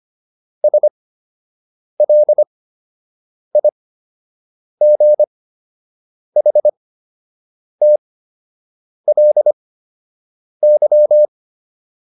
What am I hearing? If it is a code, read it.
Morse code, heard as a keyed tone, about 25 wpm, spaced as for 7 wpm, 600 Hz. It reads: SLIGHTLY